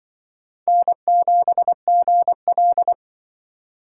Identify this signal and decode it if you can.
Morse code, heard as a keyed tone, about 23 wpm, 700 Hz. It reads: N7GL
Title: Speed Racing - USA 1x2 Call Signs 20wpm